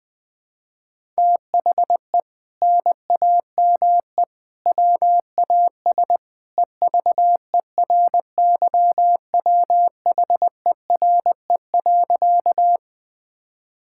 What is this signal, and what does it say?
Morse code, heard as a keyed tone, about 20 wpm, 700 Hz. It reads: THE NAME WAS EVERYWHERE.
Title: Sentences from Top 1600 Words 20wpm